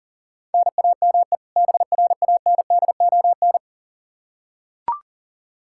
Morse code, heard as a keyed tone, about 40 wpm, 700 Hz. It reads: NAME BRANDON
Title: K1USN SST QSO Practice 40wpm